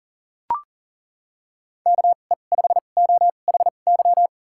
Morse code, heard as a keyed tone, about 40 wpm, spaced as for 29 wpm, 700 Hz. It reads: KE5OHY
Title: USA Call Signs - 6 Characters - 2x Character Spacing - 40wpm